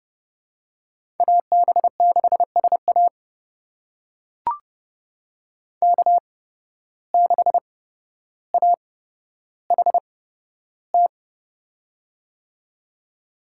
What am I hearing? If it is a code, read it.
Morse code, heard as a keyed tone, about 30 wpm, spaced as for 13 wpm, 700 Hz. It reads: AB6SA K6AHT
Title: USA Call Signs - 5 Characters - 8x Character Spacing - 30wpm